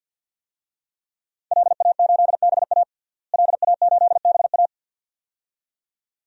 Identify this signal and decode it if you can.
Morse code, heard as a keyed tone, about 51 wpm, 700 Hz. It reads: LA8BA LA8BA
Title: DXCC-100 Call Signs - 5 Characters - Repeated 2x - 50wpm